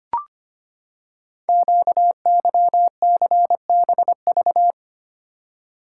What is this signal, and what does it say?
Morse code, heard as a keyed tone, about 25 wpm, 700 Hz. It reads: QYCBV
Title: Code Groups - 5 Letters - 25wpm